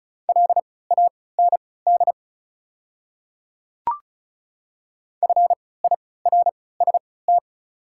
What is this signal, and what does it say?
Morse code, heard as a keyed tone, about 35 wpm, spaced as for 20 wpm, 690 Hz. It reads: LAND FIRST
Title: 3x Character Spacing - Top 200 Words - 35wpm